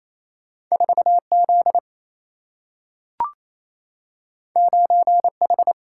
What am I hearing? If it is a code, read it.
Morse code, heard as a keyed tone, about 28 wpm, 700 Hz. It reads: 4Z 9H
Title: Single-Letter Single-Number 28wpm